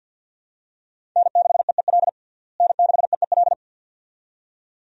Morse code, heard as a keyed tone, about 49 wpm, 700 Hz. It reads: N6EEL N6EEL
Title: USA Call Signs - 5 Characters - Repeated 2x - 50wpm